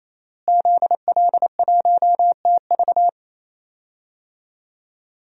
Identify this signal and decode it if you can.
Morse code, heard as a keyed tone, about 27 wpm, 700 Hz. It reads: ZL1TV
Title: DXCC-100 Call Signs - 5 Characters - Repeated 3x - 28wpm